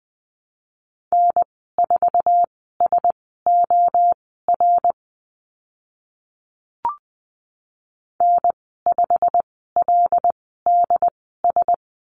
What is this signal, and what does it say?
Morse code, heard as a keyed tone, about 20 wpm, spaced as for 14 wpm, 700 Hz. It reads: N4SOR N5LDS
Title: 088 Lesson 15g - Number 5 - Call Signs 20wpm